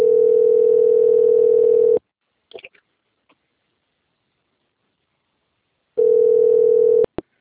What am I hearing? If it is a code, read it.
Ringback tone (North America), 440 + 480 Hz.